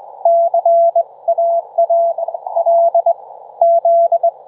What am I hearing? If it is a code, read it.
Morse code, heard as a keyed tone, about 22 wpm, spaced as for 19 wpm, 680 Hz. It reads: CAA L Z
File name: Trivia-ZL2WT